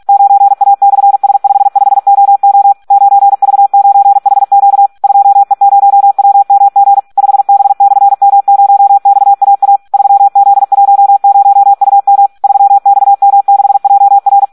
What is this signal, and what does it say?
Morse code, heard as a keyed tone, about 46 wpm, 800 Hz. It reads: 9A/S55OO 9V9HQ 2E0WMG 5B/M0XAA 3Z10UM 3XM6JR
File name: Audio practica CW / SD 1 46